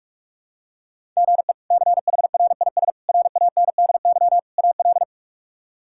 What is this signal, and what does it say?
Morse code, heard as a keyed tone, about 45 wpm, 700 Hz. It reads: GE CHRIS RANDY AL
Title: K1USN SST QSO Practice - 45wpm